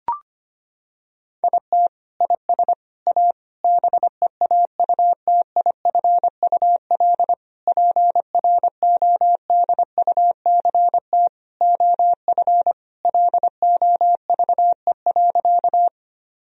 Morse code, heard as a keyed tone, about 25 wpm, 700 Hz. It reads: IT IS A BEAUTIFUL PRODUCT OF LOVE.
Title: Sentences from Top 1000 Words - Review Part 2 - 25wpm